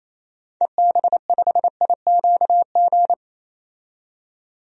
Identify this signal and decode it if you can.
Morse code, heard as a keyed tone, about 28 wpm, 700 Hz. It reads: EB5IQG
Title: DXCC-100 Call Signs - 6 Characters - 28wpm